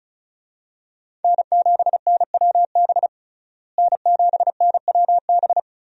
Morse code, heard as a keyed tone, about 34 wpm, 700 Hz. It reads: N7NWB N7NWB
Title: USA Call Signs - 5 Characters - Repeated 2x - 35wpm